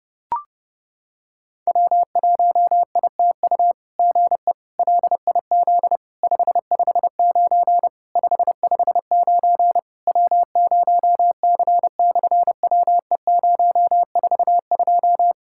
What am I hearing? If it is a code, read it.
Morse code, heard as a keyed tone, about 30 wpm, 700 Hz. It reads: W1ITU GE LIZ 559 559 W0C/WE042
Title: SOTA QSO Practice 30wpm